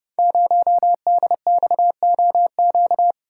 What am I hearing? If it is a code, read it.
Morse code, heard as a keyed tone, about 28 wpm, 700 Hz. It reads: 0DXOQ
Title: Code Groups - 5 Letter:Numbers - 30wpm